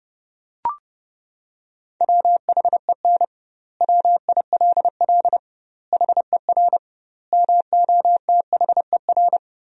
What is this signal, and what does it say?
Morse code, heard as a keyed tone, about 30 wpm, 700 Hz. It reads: WHEN WILL HER MOTHER
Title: 2x Word-Spacing - Sentences from Top 200 Words - 30wpm